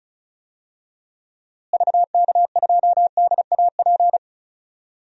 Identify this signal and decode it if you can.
Morse code, heard as a keyed tone, about 34 wpm, 700 Hz. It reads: VK2DAP
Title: DXCC-100 Call Signs - 6 Characters - Repeated 3x - 35wpm